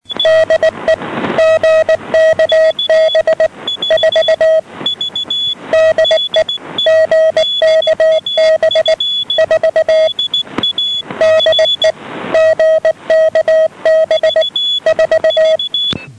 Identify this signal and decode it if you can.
Morse code, heard as a keyed tone, about 19 wpm, 640 Hz, with noise. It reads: DE GKB4 DE GKB4 DE GKB4
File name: europe / g-937-gkb4 12835.4 1130 -andystumpf